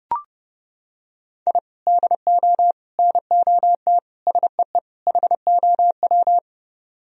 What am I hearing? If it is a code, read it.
Morse code, heard as a keyed tone, about 29 wpm, 700 Hz. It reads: I DO NOT SEE HOW